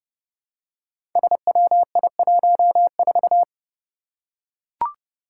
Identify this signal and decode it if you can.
Morse code, heard as a keyed tone, about 30 wpm, 700 Hz. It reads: SWI14
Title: Five Challenging Letter-Number Combinations 30wpm